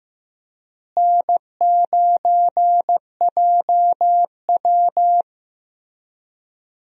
Morse code, heard as a keyed tone, about 15 wpm, 700 Hz. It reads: N9JW
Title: USA Call Signs - 4 Characters - Repeated 3x - 15wpm